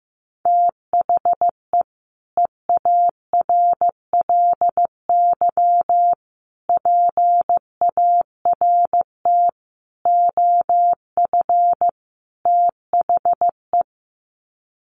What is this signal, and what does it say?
Morse code, heard as a keyed tone, about 15 wpm, 700 Hz. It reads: THE EARLY PART OF THE
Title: Sets of 5 Words Encore2 15wpm